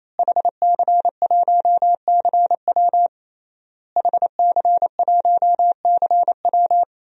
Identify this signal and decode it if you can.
Morse code, heard as a keyed tone, about 28 wpm, 700 Hz. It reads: HC1CW HC1CW
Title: DXCC-100 Call Signs - 5 Characters - Repeated 3x - 28wpm